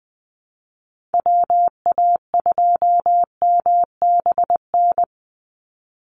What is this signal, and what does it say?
Morse code, heard as a keyed tone, about 20 wpm, 700 Hz. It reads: WA2MBN
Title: USA Call Signs 20wpm